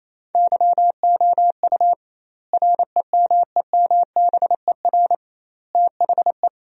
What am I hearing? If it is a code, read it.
Morse code, heard as a keyed tone, about 28 wpm, 700 Hz. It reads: YOU REMEMBER THE
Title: 2x Word-Spacing - Sentences from Top 400 Words - 28wpm